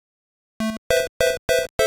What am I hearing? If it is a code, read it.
There is no code.